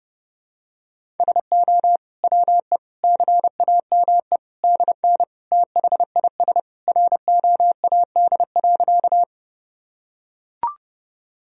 Morse code, heard as a keyed tone, about 30 wpm, 700 Hz. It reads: SO WE CAME DN THIS ROAD.
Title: Sentences from Top 400 Words and Abbreviations 30wpm